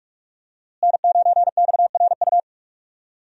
Morse code, heard as a keyed tone, about 43 wpm, 700 Hz. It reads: N9XRU
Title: USA Call Signs - 5 Characters - Repeated 3x - 45wpm